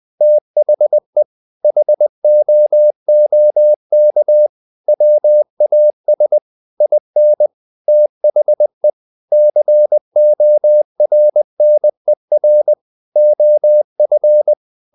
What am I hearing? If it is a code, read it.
Morse code, heard as a keyed tone, about 20 wpm, 600 Hz. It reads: THE HOOK WAS IN THE CORNER OF
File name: the old man and the sea part 2 20wpm